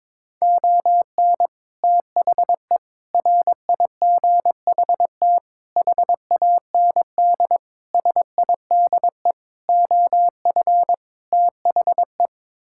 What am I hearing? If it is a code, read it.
Morse code, heard as a keyed tone, about 21 wpm, 700 Hz. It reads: ON THE RIGHT HAND SIDE OF THE
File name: practice / 22wpm / Sets of 7 Words - 22wpm